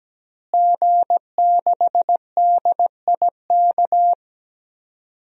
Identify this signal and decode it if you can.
Morse code, heard as a keyed tone, about 17 wpm, 700 Hz. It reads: G6DIK
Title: DXCC-100 Call Signs - 5 Characters - Repeated 3x - 17wpm